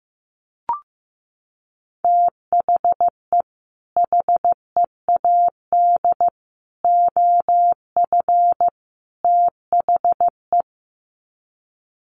Morse code, heard as a keyed tone, about 15 wpm, 700 Hz. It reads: THE HEAD OF THE
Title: Sets of 4 Words 15wpm